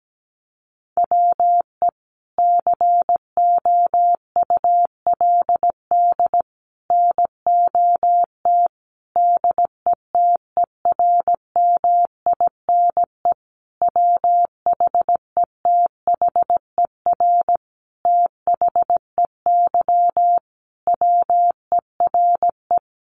Morse code, heard as a keyed tone, about 17 wpm, 700 Hz. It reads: WE COULD NOT DETERMINE WHETHER THEY WERE
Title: Sentences from Top 1300 Words 17wpm